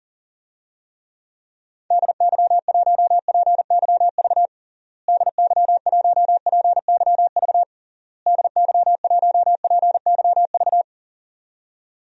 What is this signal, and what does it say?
Morse code, heard as a keyed tone, about 40 wpm, 700 Hz. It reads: DY1PYV DY1PYV DY1PYV
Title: DXCC-100 Call Signs - 6 Characters - Repeated 3x - 40wpm